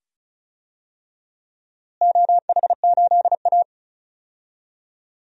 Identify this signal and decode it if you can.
Morse code, heard as a keyed tone, about 36 wpm, 700 Hz. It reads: OH8A
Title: DXCC-100 Call Signs - 4 Characters - Repeated 2x - 35wpm